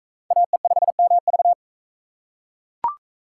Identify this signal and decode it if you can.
Morse code, heard as a keyed tone, about 36 wpm, 700 Hz. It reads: AE5MV